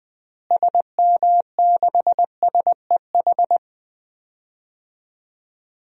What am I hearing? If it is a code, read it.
Morse code, heard as a keyed tone, about 20 wpm, 700 Hz. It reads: SM6SEH
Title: DXCC-100 Call Signs - 6 Characters - Repeated 2x - 20wpm